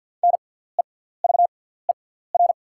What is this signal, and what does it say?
Morse code, heard as a keyed tone, about 48 wpm, spaced as for 17 wpm, 690 Hz. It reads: NEVER
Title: 6x Character Spacing - Top 300 Words - 50wpm